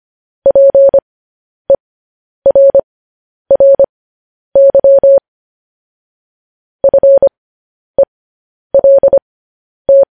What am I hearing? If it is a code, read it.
Morse code, heard as a keyed tone, about 25 wpm, spaced as for 10 wpm, 550 Hz. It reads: PERRY FELT
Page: news headlines in morse code at 10 wpm